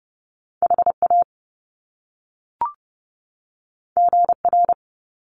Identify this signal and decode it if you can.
Morse code, heard as a keyed tone, about 30 wpm, 700 Hz. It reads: HA GR
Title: Top Two Letter Combinations in Words 30wpm